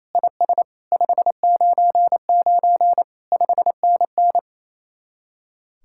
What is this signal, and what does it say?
Morse code, heard as a keyed tone, about 34 wpm, 700 Hz. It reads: IS 599 5NN
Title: Speed-Racing Common QSO Phrases - 28wpm